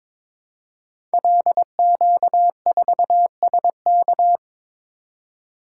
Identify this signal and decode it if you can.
Morse code, heard as a keyed tone, about 22 wpm, 700 Hz. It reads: LQ4SK
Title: Code Groups - 5 Letter:Numbers - 22wpm